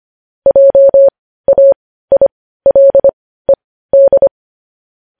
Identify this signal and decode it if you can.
Morse code, heard as a keyed tone, about 25 wpm, spaced as for 15 wpm, 550 Hz. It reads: JAILED
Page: news-headlines-in-morse-code-at-15-wpm